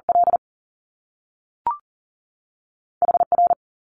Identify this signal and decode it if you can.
Morse code, heard as a keyed tone, about 40 wpm, spaced as for 3 wpm, 700 Hz. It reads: LH R
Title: Top Two Letter Combinations in Words 40wpm